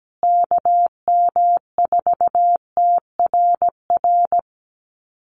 Morse code, heard as a keyed tone, about 17 wpm, 700 Hz. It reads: KM4TRR
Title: USA Call Signs Repeated 2x 17wpm